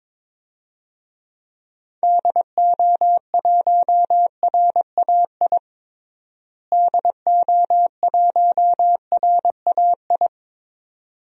Morse code, heard as a keyed tone, about 22 wpm, 700 Hz. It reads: DO1RAI DO1RAI
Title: DXCC-100 Call Signs - 6 Characters - Repeated 2x - 22wpm